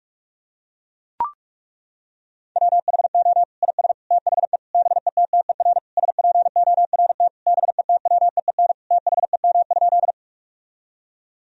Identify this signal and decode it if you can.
Morse code, heard as a keyed tone, about 41 wpm, 700 Hz. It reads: WHO IS THE BETTER SPORT BETWEEN THEM?
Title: Sentences from Top 1000 Words - Review Part 4 - 45wpm